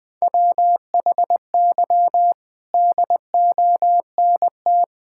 Morse code, heard as a keyed tone, about 20 wpm, 700 Hz. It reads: WHY DONT
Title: Sentences from Top 1000 Words - Review Part 5 - 20wpm